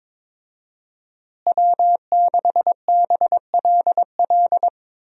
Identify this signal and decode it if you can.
Morse code, heard as a keyed tone, about 22 wpm, 700 Hz. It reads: W6BLL
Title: USA Call Signs - 5 Characters - Repeated 2x - 22wpm